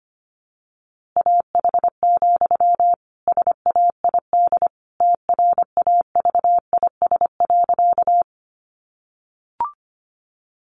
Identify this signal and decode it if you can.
Morse code, heard as a keyed tone, about 25 wpm, 700 Hz. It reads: AH, SAID TRAVIS.
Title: A Sound Of Thunder 25wpm